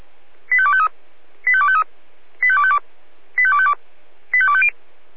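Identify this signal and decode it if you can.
Five-tone selective call (ZVEI), heard as quick runs of five tones, tones of about 70 ms. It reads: 85253 85253 85252 85252 85250
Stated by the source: ZVEI five-tone selective call received on an SDR.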